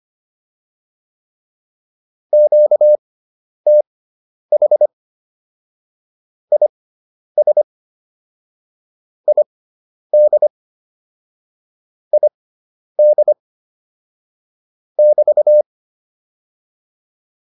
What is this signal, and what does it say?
Morse code, heard as a keyed tone, about 25 wpm, spaced as for 10 wpm, 600 Hz. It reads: QTH IS ID ID <BT>